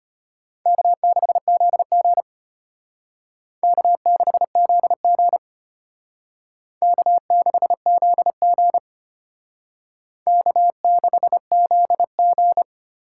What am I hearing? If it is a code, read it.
Morse code, heard as a keyed tone, about 32 wpm, 700 Hz. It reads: K6ZG K6ZG K6ZG K6ZG